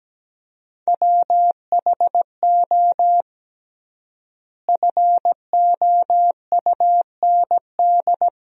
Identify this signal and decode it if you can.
Morse code, heard as a keyed tone, about 17 wpm, 700 Hz. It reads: WHO FOUND
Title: 3x Word-Spacing - Sentences from Top 300 Words - 17wpm